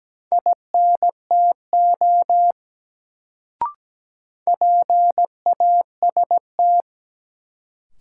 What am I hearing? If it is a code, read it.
Morse code, heard as a keyed tone, about 17 wpm, 700 Hz. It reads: INTO PAST
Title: Sets of 2 Words - New Word Drill - 17wpm